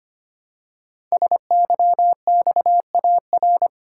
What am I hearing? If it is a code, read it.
Morse code, heard as a keyed tone, about 25 wpm, 700 Hz. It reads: SYXAR
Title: Code Groups - 5 Letters - 25wpm